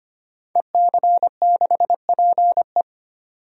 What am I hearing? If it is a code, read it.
Morse code, heard as a keyed tone, about 25 wpm, 700 Hz. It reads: EC6PE
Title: DXCC-100 Call Signs - 5 Characters - 25wpm